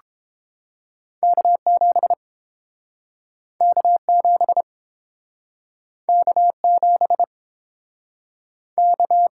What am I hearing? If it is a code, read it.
Morse code, heard as a keyed tone, about 30 wpm, 700 Hz. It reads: K 7 K 7 K 7 K